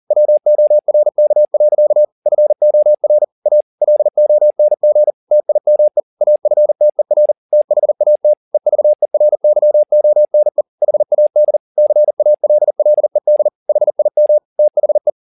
Morse code, heard as a keyed tone, about 40 wpm, 600 Hz. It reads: WORK. FOR A LONG TIME AFTER THAT EVERYONE HAD CALLED HIM THE
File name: the old man and the sea 2 40wpm